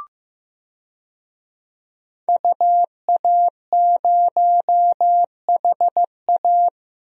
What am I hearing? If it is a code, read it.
Morse code, heard as a keyed tone, about 15 wpm, 700 Hz. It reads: UA0HA
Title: DXCC-100 Call Signs - 5 Characters - Repeated 3x - 15wpm